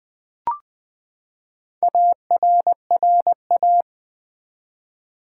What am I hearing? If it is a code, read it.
Morse code, heard as a keyed tone, about 20 wpm, 700 Hz. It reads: ARRA
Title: Build A Word - 4 Letters At A Time - 11-Letter Words - 20wpm